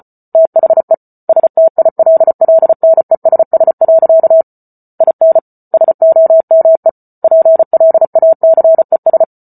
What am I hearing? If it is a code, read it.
Morse code, heard as a keyed tone, about 34 wpm, 650 Hz. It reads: THE STILLNESS. IN SOME PLACES